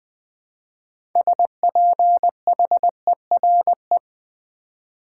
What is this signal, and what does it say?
Morse code, heard as a keyed tone, about 20 wpm, 700 Hz. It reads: SPHERE
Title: Build A Word - 2 Letters At A Time - 6-Letter Words - 20wpm